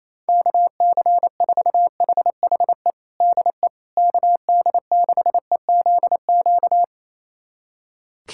Morse code, heard as a keyed tone, about 27 wpm, 700 Hz. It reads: KC4HHE DE KD6EZQ
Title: USA Callsign DE Callsign - 6 Characters - 28wpm